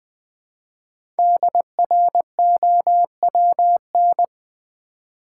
Morse code, heard as a keyed tone, about 20 wpm, 700 Hz. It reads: DROWN
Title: Build A Word - 1 Letter At A Time - 5-Letter Words - 20wpm